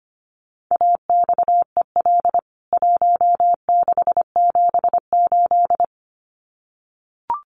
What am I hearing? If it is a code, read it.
Morse code, heard as a keyed tone, about 25 wpm, 700 Hz. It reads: AXEL 1678
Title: CWT Exchange 25wpm